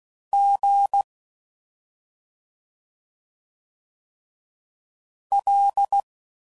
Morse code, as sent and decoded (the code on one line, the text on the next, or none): --. / .-..
G L